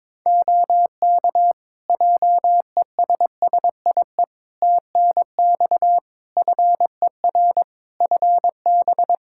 Morse code, heard as a keyed tone, about 22 wpm, 700 Hz. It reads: OK JESSIE TNX FER FB
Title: Practice QSO - Exchange 4 - Ending - 22wpm